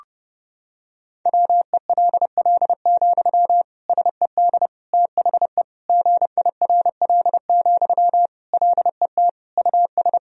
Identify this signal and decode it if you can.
Morse code, heard as a keyed tone, about 30 wpm, 700 Hz. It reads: WELL, SED THE GIRL, LET US
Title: Sentences from Top 400 Words and Abbreviations 30wpm